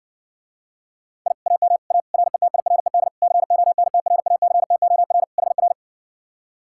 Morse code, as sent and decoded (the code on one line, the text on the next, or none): .. / .- -- / .- / .-. - .. .-. -.. / -.-. --- -. - .-. .- -.-. - --- .-. / .... .-.
I AM A RETIRED CONTRACTOR HR